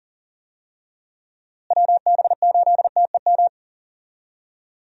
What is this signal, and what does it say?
Morse code, heard as a keyed tone, about 40 wpm, 700 Hz. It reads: WB8TEM
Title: USA Call Signs - 6 Characters - Repeated 3x - 40wpm